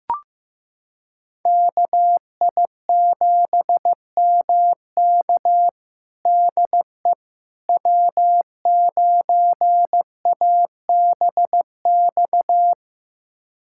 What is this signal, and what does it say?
Morse code, heard as a keyed tone, about 15 wpm, 700 Hz. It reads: KI7MK DE W9ABX
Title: USA Callsign DE Callsign - 5 Characters - 15wpm